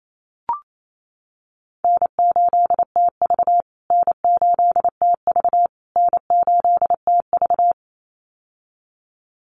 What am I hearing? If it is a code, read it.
Morse code, heard as a keyed tone, about 28 wpm, 700 Hz. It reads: N8TV N8TV N8TV
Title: USA Call Signs Repeated 3x 28wpm